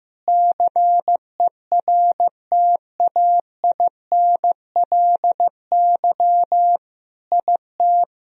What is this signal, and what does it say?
Morse code, heard as a keyed tone, about 15 wpm, 700 Hz. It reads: CERTAINLY IT
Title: Sentences from Top 1000 Words - Review Part 1 - 15wpm